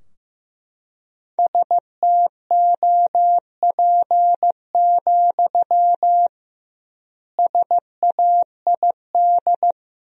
Morse code, heard as a keyed tone, about 15 wpm, 700 Hz. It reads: STOP, SAID
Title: 2x Word-Spacing - Sentences from Top 300 Words - 15wpm